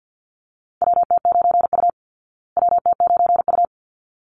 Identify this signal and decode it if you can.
Morse code, heard as a keyed tone, about 50 wpm, 700 Hz. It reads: WT9U WT9U
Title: Call Signs Repeated 3x - 50wpm